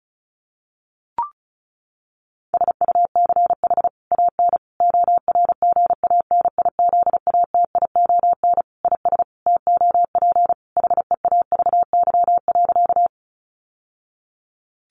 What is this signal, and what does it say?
Morse code, heard as a keyed tone, about 34 wpm, 700 Hz. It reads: SUCH AN ORGANIZATION IS TOP HEAVY.
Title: Sentences from Top 1100 Words 35wpm